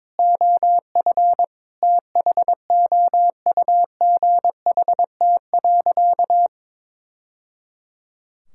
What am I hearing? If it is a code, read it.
Morse code, heard as a keyed tone, about 22 wpm, 700 Hz. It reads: OF THOUGHT.
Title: Sentences from Top 1000 Words - Review Part 5 - 22wpm